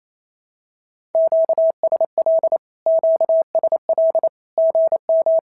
Morse code, heard as a keyed tone, about 28 wpm, 650 Hz. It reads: QSL QSL GM
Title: SOTA QSO Practice 28wpm